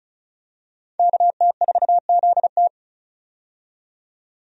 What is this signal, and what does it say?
Morse code, heard as a keyed tone, about 34 wpm, 700 Hz. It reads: KT4ZT
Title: USA Call Signs - 5 Characters - 35wpm